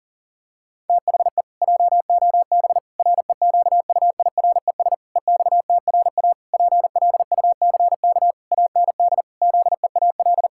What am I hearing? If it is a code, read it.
Morse code, heard as a keyed tone, about 37 wpm, 700 Hz. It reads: THE JOB REQUIRES EXTRA PLUCK AND ZEAL
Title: Pangram Challenge v2 - 40wpm